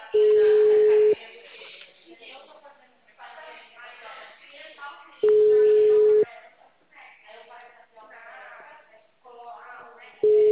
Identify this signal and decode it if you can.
Ringback tone (Europe, 425 Hz), with speech.